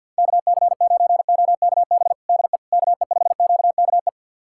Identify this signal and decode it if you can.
Morse code, heard as a keyed tone, about 50 wpm, 690 Hz. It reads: KC9OKB DE KE5QKE
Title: USA Callsign DE Callsign - 6 Characters - 50wpm